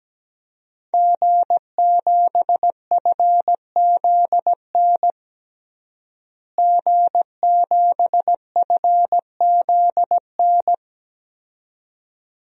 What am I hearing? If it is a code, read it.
Morse code, heard as a keyed tone, about 17 wpm, 700 Hz. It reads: G7FZN G7FZN
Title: DXCC-100 Call Signs - 5 Characters - Repeated 2x - 17wpm